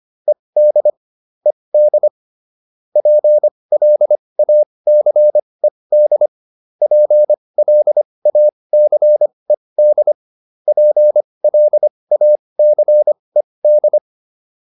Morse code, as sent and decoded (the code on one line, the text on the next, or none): . -.. / . -.. / .--. .-.. .- -.-. -.. / .--. .-.. .- -.-. -.. / .--. .-.. .- -.-. -..
ED ED PLACED PLACED PLACED